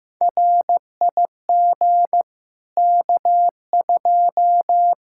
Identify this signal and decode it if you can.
Morse code, heard as a keyed tone, about 15 wpm, 700 Hz. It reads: RIG K2
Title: Practice QSO - Exchange 2 - Info - 15wpm